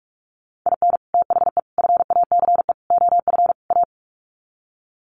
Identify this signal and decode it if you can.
Morse code, heard as a keyed tone, about 44 wpm, 700 Hz. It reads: IN THE FACE OF A
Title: Sets of 5 Words Encore 45wpm